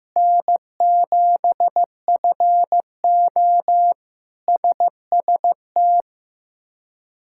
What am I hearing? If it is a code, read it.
Morse code, heard as a keyed tone, about 15 wpm, 700 Hz. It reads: N7FO SST
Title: K1USN SST QSO Practice - 15wpm